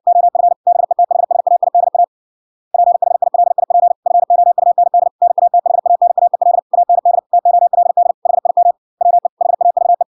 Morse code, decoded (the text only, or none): OF BETHSAIDA PHILIP FOUND NATHANAEL AND TOLD HIM WE HAVE